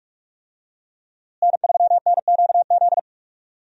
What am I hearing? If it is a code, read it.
Morse code, heard as a keyed tone, about 42 wpm, 700 Hz. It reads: N3NQZ
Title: USA Call Signs - 5 Characters - Repeated 3x - 45wpm